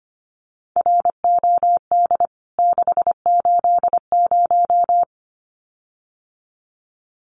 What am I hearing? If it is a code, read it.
Morse code, heard as a keyed tone, about 25 wpm, 700 Hz. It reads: ROD 680
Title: CWT Exchange 25wpm